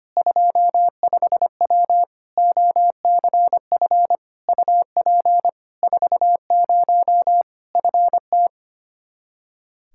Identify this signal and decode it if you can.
Morse code, heard as a keyed tone, about 25 wpm, 700 Hz. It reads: 25W OCF UP 40 FT
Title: Practice QSO - Exchange 2 - Info - 25wpm